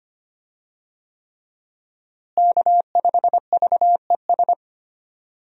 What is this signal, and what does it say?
Morse code, heard as a keyed tone, about 25 wpm, 700 Hz. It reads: K5VES
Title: USA Call Signs - 5 Characters - Repeated 3x - 25wpm